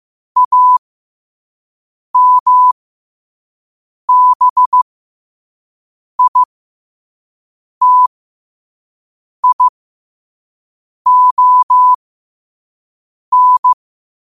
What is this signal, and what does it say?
Morse code, heard as a keyed tone, about 15 wpm, spaced as for 5 wpm, 1000 Hz. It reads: AMBITION
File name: qotdfiles / 5wpm